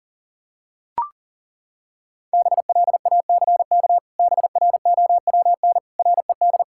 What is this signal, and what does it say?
Morse code, heard as a keyed tone, about 37 wpm, 700 Hz. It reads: BLACK BROWN RED
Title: Pangram Challenge v2 - 40wpm